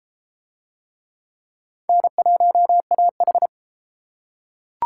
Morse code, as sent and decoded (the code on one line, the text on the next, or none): -. .---- .- ....
N1AH